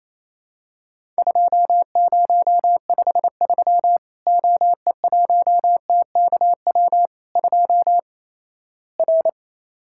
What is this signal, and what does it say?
Morse code, heard as a keyed tone, about 28 wpm, 700 Hz. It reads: 2053 OE1TKW 2 R
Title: Worked All Europe DX Contest - QTC Exchange - 28wpm